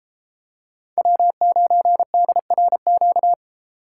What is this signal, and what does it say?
Morse code, heard as a keyed tone, about 33 wpm, 700 Hz. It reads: W9DRQ